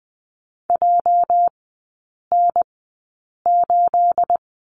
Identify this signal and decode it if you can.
Morse code, heard as a keyed tone, about 20 wpm, spaced as for 8 wpm, 700 Hz. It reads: JN8
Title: 212 Lesson 34e - Number 8 - Three Characters 20wpm